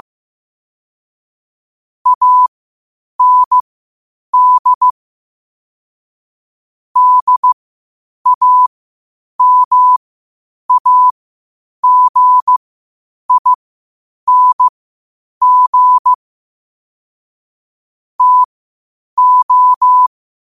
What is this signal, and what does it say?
Morse code, heard as a keyed tone, about 15 wpm, spaced as for 8 wpm, 1000 Hz. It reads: AND DAMAGING TO
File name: qotdfiles / 8wpm